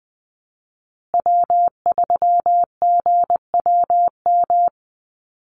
Morse code, heard as a keyed tone, about 20 wpm, 700 Hz. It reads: W3GWM
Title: Call Signs Repeated 2x - 20wpm